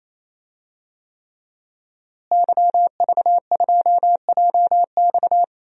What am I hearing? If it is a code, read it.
Morse code, heard as a keyed tone, about 28 wpm, 700 Hz. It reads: YV2JX
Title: DXCC-100 Call Signs - 5 Characters - Repeated 3x - 28wpm